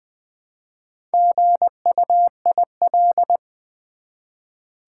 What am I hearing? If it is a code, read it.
Morse code, heard as a keyed tone, about 20 wpm, 700 Hz. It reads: GUIL
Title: Build A Word - 1 Letter At A Time - 5-Letter Words - 20wpm